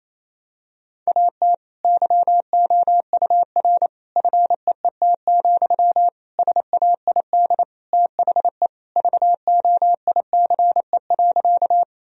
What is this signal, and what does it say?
Morse code, heard as a keyed tone, about 28 wpm, 700 Hz. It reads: AT YOUR FEET, SAID THE VOICE.
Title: Sentences from Top 1000 Words - Review Part 1 - 28wpm